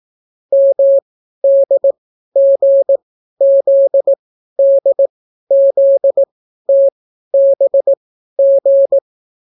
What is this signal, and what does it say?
Morse code, heard as a keyed tone, about 18 wpm, spaced as for 12 wpm, 550 Hz. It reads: MDGZDZTBG